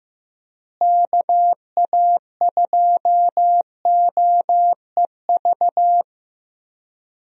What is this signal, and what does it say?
Morse code, heard as a keyed tone, about 15 wpm, 700 Hz. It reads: KA2OEV
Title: USA Call Signs - 6 Characters - Repeated 2x - 15wpm